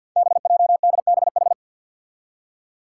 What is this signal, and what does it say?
Morse code, heard as a keyed tone, about 50 wpm, 700 Hz. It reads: BJDBH
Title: Five Challenging Letter-Number Combinations 50wpm